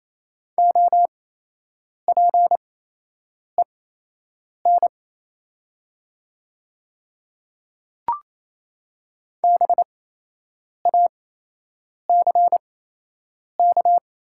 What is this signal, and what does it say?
Morse code, heard as a keyed tone, about 28 wpm, spaced as for 8 wpm, 700 Hz. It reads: OPEN BACK
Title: 8x Character Spacing - Top 300 Words - 28wpm